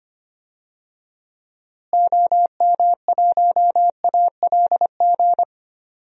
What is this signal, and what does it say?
Morse code, heard as a keyed tone, about 25 wpm, 700 Hz. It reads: OM1ALG
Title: DXCC-100 Call Signs - 6 Characters - Repeated 2x - 25wpm